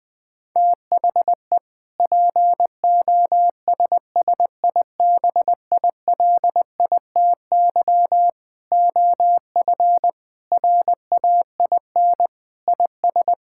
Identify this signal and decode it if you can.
Morse code, heard as a keyed tone, about 20 wpm, 700 Hz. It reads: THE POSSIBILITY OF RAIN IS
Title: Sentences with 3-Word Phrases - 20wpm